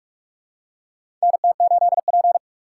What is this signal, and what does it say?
Morse code, heard as a keyed tone, about 41 wpm, 700 Hz. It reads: NT8P
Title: USA Call Signs - 4 Characters - 2x Character Spacing - 45wpm